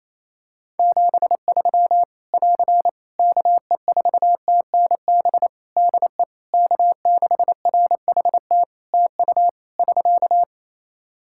Morse code, heard as a keyed tone, about 28 wpm, 700 Hz. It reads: 73 <AR> KE4TNB DE K6RHT TU <SK>
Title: Practice QSO - Exchange 4 - Ending - 28wpm